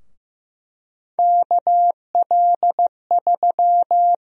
Morse code, heard as a keyed tone, about 15 wpm, 700 Hz. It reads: KL3
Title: USA Call Sign Prefixes - Repeated 2x - 15wpm